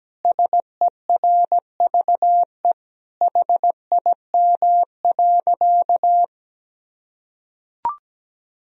Morse code, heard as a keyed tone, about 17 wpm, 700 Hz. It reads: SERVE HIM.
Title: Sentences from Top 1000 Words - Review Part 2 - 17wpm